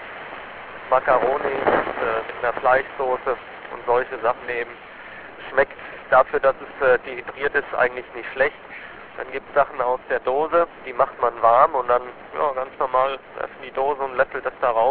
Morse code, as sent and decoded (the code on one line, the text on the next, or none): none